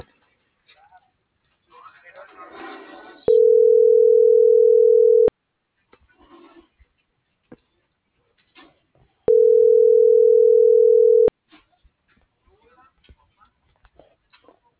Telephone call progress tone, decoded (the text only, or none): ringback tone